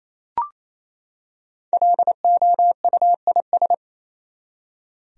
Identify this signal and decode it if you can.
Morse code, heard as a keyed tone, about 28 wpm, 700 Hz. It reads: LOUIS
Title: USA Common Names - Max 5 Letters - 28wpm